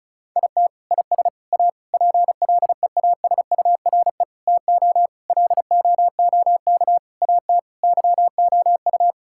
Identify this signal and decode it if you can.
Morse code, heard as a keyed tone, about 36 wpm, 700 Hz. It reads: IT IS A PLEASURE TO LOOK AT YOU
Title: Sentences from Top 1600 Words 35wpm